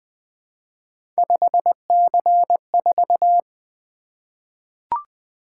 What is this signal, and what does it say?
Morse code, heard as a keyed tone, about 20 wpm, 700 Hz. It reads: 5C4
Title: Three Letter-Number Combinations - 20wpm